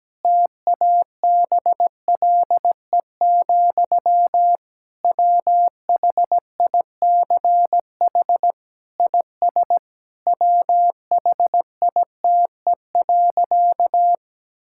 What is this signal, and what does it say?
Morse code, heard as a keyed tone, about 17 wpm, 700 Hz. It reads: TABLE, WHICH IS WHITE.
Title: Sentences from Top 1600 Words 17wpm